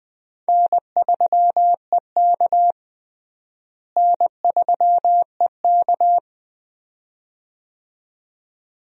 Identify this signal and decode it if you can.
Morse code, heard as a keyed tone, about 20 wpm, 700 Hz. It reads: N3EK N3EK
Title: USA Call Signs - 4 Characters - Repeated 2x - 20wpm